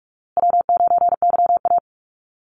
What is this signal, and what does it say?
Morse code, heard as a keyed tone, about 44 wpm, 700 Hz. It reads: W9YA
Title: Easy Call Signs Repeated 3x - 45wpm